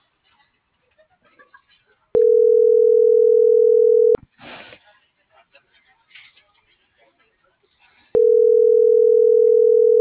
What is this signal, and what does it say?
Ringback tone (North America), 440 + 480 Hz.
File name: CENTRAL MULTIMODAL / out josimar.cruz - .88413